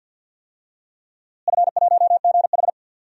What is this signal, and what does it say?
Morse code, heard as a keyed tone, about 50 wpm, 700 Hz. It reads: F1GH
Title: DXCC-100 Call Signs - 4 Characters - Repeated 2x - 50wpm